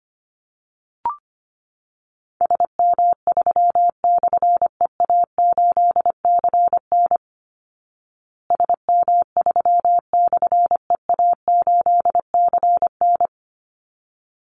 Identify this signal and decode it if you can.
Morse code, heard as a keyed tone, about 25 wpm, 700 Hz. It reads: SM3/EA8CN SM3/EA8CN
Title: Call Signs Repeated 3x - 25wpm